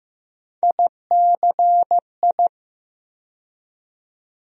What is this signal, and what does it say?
Morse code, heard as a keyed tone, about 15 wpm, 700 Hz. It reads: ICI
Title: USA Call Sign Postfixes - 3 Letters 15wpm